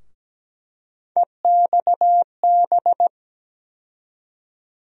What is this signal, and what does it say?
Morse code, heard as a keyed tone, about 17 wpm, 700 Hz. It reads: EXB
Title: USA Call Sign Postfixes - 3 Letters 17wpm